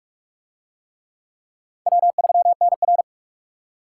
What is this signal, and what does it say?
Morse code, heard as a keyed tone, about 40 wpm, 700 Hz. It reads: W3NR